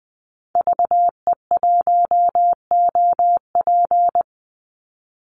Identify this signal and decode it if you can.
Morse code, heard as a keyed tone, about 20 wpm, 700 Hz. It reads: VE1OP
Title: Call Signs Repeated 3x - 20wpm